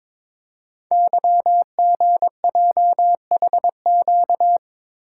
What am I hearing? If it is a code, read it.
Morse code, heard as a keyed tone, about 23 wpm, 700 Hz. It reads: YGJHQ
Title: Code Groups - 5 Letters - 22wpm